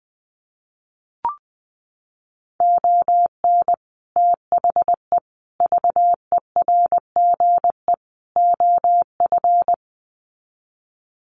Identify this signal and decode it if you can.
Morse code, heard as a keyed tone, about 20 wpm, 700 Hz. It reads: ON THE VERGE OF
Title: Sets of 4 Words 20wpm